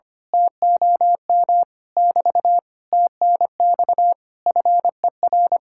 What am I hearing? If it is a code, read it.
Morse code, heard as a keyed tone, about 24 wpm, 700 Hz. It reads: TOM <BT> TNX FER
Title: Speed-Racing Common QSO Phrases Encore - 25wpm